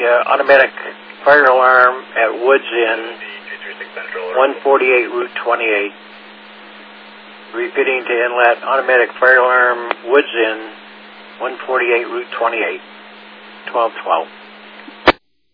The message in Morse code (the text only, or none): none